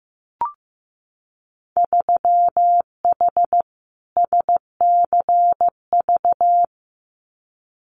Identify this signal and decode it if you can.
Morse code, heard as a keyed tone, about 15 wpm, 700 Hz. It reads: 3H SCV
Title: Winter Field Day Exchanges Encore 15wpm